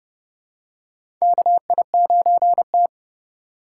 Morse code, heard as a keyed tone, about 29 wpm, 700 Hz. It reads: KI9T
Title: USA Call Signs - 4 Characters - 3x Character Spacing - 30wpm